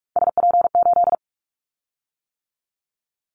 Morse code, heard as a keyed tone, about 46 wpm, 700 Hz. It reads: SP8
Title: Callsign Prefixes for Top 25 DXCC - 45wpm